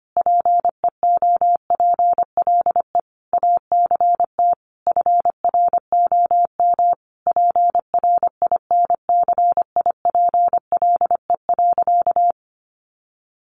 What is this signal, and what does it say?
Morse code, heard as a keyed tone, about 25 wpm, 700 Hz. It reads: PEOPLE ACT FROM PRINCIPLE.
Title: Sentences from Top 1400 Words 25wpm